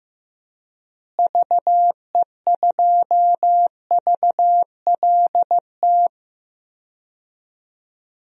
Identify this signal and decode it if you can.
Morse code, heard as a keyed tone, about 15 wpm, 700 Hz. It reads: VE2VLT
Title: DXCC-100 Call Signs - 6 Characters - Repeated 2x - 15wpm